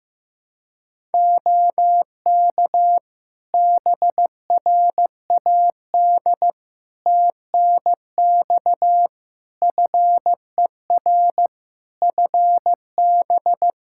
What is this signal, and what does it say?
Morse code, heard as a keyed tone, about 15 wpm, 700 Hz. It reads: OK BRAD TNX FER FB
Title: K1USN SST QSO Practice 15wpm